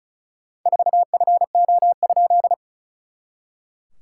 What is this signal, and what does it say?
Morse code, heard as a keyed tone, about 34 wpm, 700 Hz. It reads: 4FO?
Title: SOTA QSO Practice 35wpm